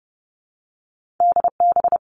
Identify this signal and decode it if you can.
Morse code, heard as a keyed tone, about 30 wpm, 700 Hz. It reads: DB
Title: All Cut Numbers - 2-Digit 30wpm